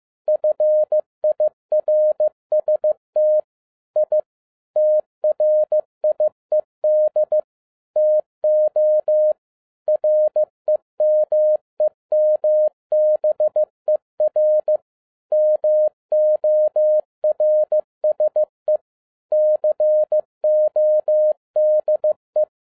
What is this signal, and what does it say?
Morse code, heard as a keyed tone, about 15 wpm, 610 Hz. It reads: FIRST I TRIED TO REMEMBER MORSE CODE
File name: MY CW LIFE 1 15wpm